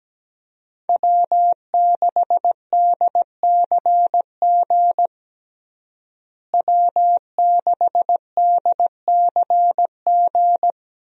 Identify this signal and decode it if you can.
Morse code, heard as a keyed tone, about 17 wpm, 700 Hz. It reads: W6DCG W6DCG
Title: USA Call Signs - 5 Characters - Repeated 2x - 17wpm